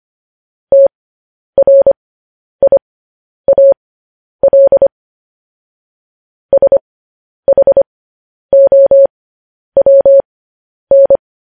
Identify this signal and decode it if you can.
Morse code, heard as a keyed tone, about 25 wpm, spaced as for 10 wpm, 550 Hz. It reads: TRIAL SHOWN